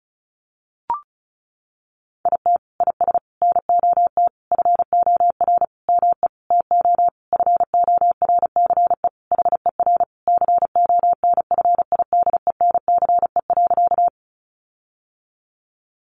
Morse code, heard as a keyed tone, about 34 wpm, 700 Hz. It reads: IT IS NOT FOR ME TO FORCE HER CONFIDENCE.
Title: Sentences from Top 1500 Words 35wpm